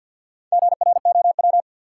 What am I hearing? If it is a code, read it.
Morse code, heard as a keyed tone, about 47 wpm, 700 Hz. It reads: GROW
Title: Sentences from Top 300 Words - New Word Drill - 50wpm